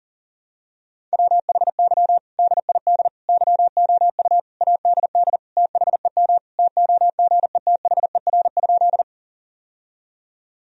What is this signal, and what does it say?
Morse code, heard as a keyed tone, about 40 wpm, 700 Hz. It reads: WHY DID YOU ADD THEM TOGETHER?
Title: Sentences from Top 1000 Words - Review Part 1 - 40wpm